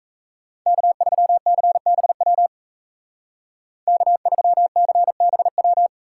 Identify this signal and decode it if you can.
Morse code, heard as a keyed tone, about 37 wpm, 700 Hz. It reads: K3CBW K3CBW